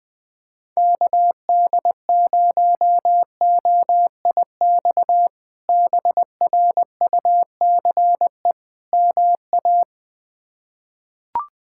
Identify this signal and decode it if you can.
Morse code, heard as a keyed tone, about 20 wpm, 700 Hz. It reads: KD0OIX BRUCE MA
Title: K1USN SST QSO Practice - 20wpm